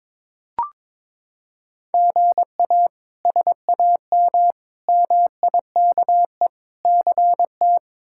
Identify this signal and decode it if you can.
Morse code, heard as a keyed tone, about 22 wpm, 700 Hz. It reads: GA SAM MIKE CT
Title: K1USN SST QSO Practice - 22wpm